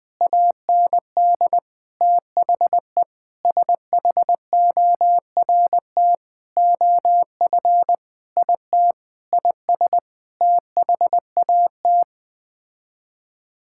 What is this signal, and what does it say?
Morse code, heard as a keyed tone, about 21 wpm, 700 Hz. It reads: AND THE SHORT OF IT IS THAT
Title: Sets of 9 Words - 20wpm